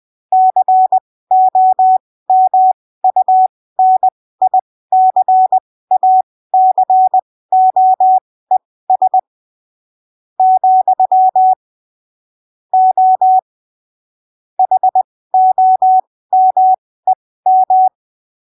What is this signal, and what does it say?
Morse code, heard as a keyed tone, about 20 wpm, spaced as for 15 wpm, 750 Hz. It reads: COMUNICACOES , O HOMEM
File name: arquivos / texto / texto 23 20ppm